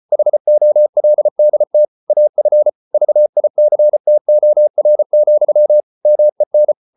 Morse code, heard as a keyed tone, about 34 wpm, 600 Hz. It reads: HOLDT AF VICTOR, MEN
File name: Ferie QRQ 34wpm